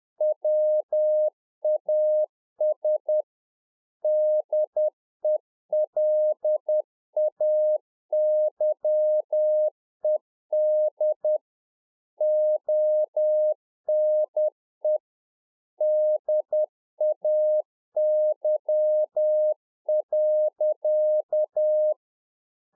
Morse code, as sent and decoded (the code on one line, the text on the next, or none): .-- .- ... / -.. .-.. .- -.-- . -.. / --- -. / -.. .- -.-- .-.-.-
WAS DELAYED ONE DAY.